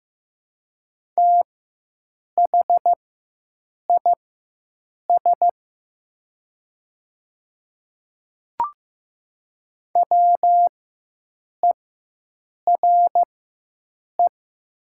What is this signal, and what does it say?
Morse code, heard as a keyed tone, about 15 wpm, spaced as for 7 wpm, 700 Hz. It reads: THIS WERE
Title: 4x Character Spacing - Top 100 Words - 15wpm